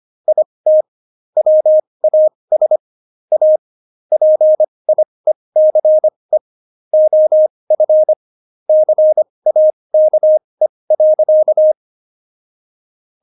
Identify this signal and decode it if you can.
Morse code, heard as a keyed tone, about 25 wpm, spaced as for 20 wpm, 620 Hz. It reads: IT WAS A PIECE OF CAKE.
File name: Jokes-25-char-20-eff